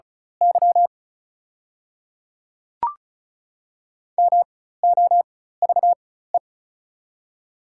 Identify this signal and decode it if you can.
Morse code, heard as a keyed tone, about 36 wpm, spaced as for 16 wpm, 690 Hz. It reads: Y MOVE
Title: 4x Character Spacing - Top 300 Words - 35wpm